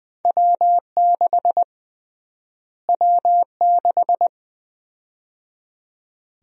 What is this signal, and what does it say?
Morse code, heard as a keyed tone, about 20 wpm, 700 Hz. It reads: W6 W6
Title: USA Call Sign Prefixes - Repeated 2x - 20wpm